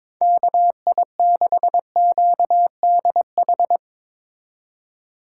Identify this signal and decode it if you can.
Morse code, heard as a keyed tone, about 22 wpm, 700 Hz. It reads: KI6QDH
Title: K1USN SST QSO Practice - 22wpm